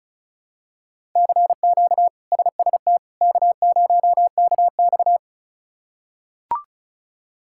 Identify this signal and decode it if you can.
Morse code, heard as a keyed tone, about 36 wpm, 700 Hz. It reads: CQ SST K0KX
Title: K1USN SST QSO Practice - 35wpm